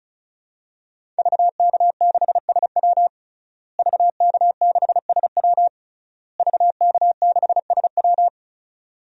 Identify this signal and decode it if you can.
Morse code, heard as a keyed tone, about 36 wpm, 700 Hz. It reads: VK6SW VK6SW VK6SW
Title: DXCC-100 Call Signs - 5 Characters - Repeated 3x - 35wpm